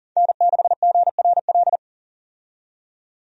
Morse code, heard as a keyed tone, about 40 wpm, 700 Hz. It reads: N6GRL
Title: USA Call Signs - 5 Characters - 40wpm